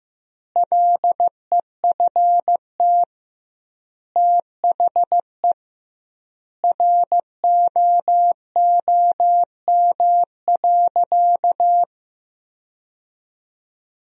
Morse code, heard as a keyed tone, about 15 wpm, 700 Hz. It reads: LEFT THE ROOM.
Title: 2x Word-Spacing - Sentences from Top 300 Words - 15wpm